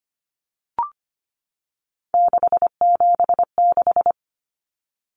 Morse code, heard as a keyed tone, about 25 wpm, 700 Hz. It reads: 676
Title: Numbers - 3-Digit 25wpm